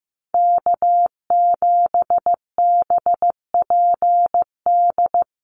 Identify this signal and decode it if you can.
Morse code, heard as a keyed tone, about 15 wpm, 700 Hz. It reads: K7BPD
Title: USA Call Signs 15wpm